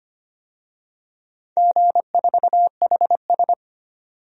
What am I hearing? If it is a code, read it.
Morse code, heard as a keyed tone, about 25 wpm, 700 Hz. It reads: G4HS
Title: DXCC-100 Call Signs - 4 Characters - 25wpm